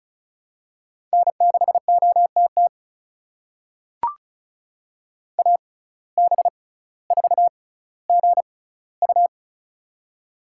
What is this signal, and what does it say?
Morse code, heard as a keyed tone, about 36 wpm, spaced as for 18 wpm, 700 Hz. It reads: N6OTT AB4GU
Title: USA Call Signs - 5 Characters - 6x Character Spacing - 35wpm